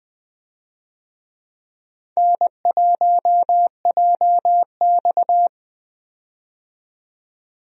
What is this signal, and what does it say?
Morse code, heard as a keyed tone, about 20 wpm, 700 Hz. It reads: N1JX
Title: USA Call Signs - 4 Characters - 20wpm